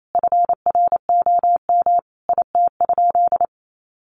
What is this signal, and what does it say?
Morse code, heard as a keyed tone, about 28 wpm, 700 Hz. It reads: FROM IT?
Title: Sentences from Top 800 Words 28wpm